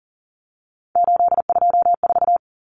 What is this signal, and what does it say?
Morse code, heard as a keyed tone, about 40 wpm, 700 Hz. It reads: 824
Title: Common Cut Numbers - 3-Digit 40wpm